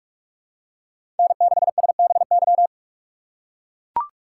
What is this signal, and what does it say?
Morse code, heard as a keyed tone, about 41 wpm, 700 Hz. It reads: N6SBY
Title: USA Call Signs - 5 Characters - 2x Character Spacing - 45wpm